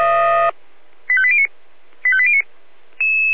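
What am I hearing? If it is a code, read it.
Five-tone selective call (ZVEI), heard as quick runs of five tones, tones of about 70 ms. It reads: 85809 85809
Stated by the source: ZVEI five-tone call 85830 received by SDR.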